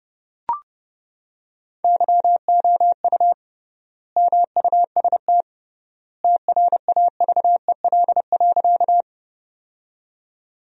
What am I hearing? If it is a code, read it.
Morse code, heard as a keyed tone, about 32 wpm, 700 Hz. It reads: YOU MUST TRAVEL.
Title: 3x Word-Spacing - Sentences from Top 400 Words - 30wpm